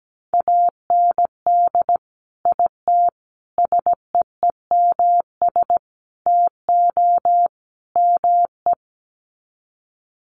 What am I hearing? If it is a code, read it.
Morse code, heard as a keyed tone, about 17 wpm, 700 Hz. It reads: AND IT SEEMS TO ME
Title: Sets of 5 Words Encore2 17wpm